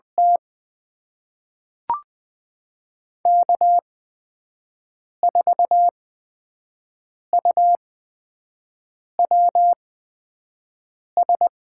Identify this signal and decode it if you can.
Morse code, heard as a keyed tone, about 20 wpm, spaced as for 5 wpm, 700 Hz. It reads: T K4UWS
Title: USA Call Signs - 8x Character Spacing - 20wpm